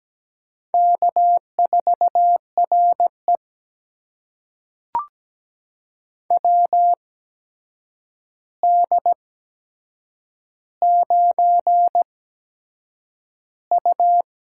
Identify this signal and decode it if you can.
Morse code, heard as a keyed tone, about 17 wpm, spaced as for 7 wpm, 700 Hz. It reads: K4RE WD9U
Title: USA Call Signs - 4 Characters - 8x Character Spacing - 17wpm